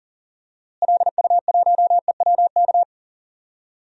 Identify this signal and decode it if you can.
Morse code, heard as a keyed tone, about 40 wpm, 700 Hz. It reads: LU1EWK
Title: DXCC-100 Call Signs - 6 Characters - Repeated 3x - 40wpm